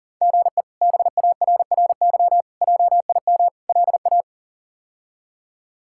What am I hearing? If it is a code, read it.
Morse code, heard as a keyed tone, about 40 wpm, 700 Hz. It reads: GE BARRY JIM LA
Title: K1USN SST QSO Practice - 40wpm